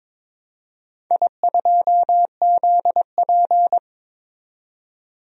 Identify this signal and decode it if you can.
Morse code, heard as a keyed tone, about 21 wpm, 700 Hz. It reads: I2ZP
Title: DXCC-100 Call Signs - 4 Characters - 22wpm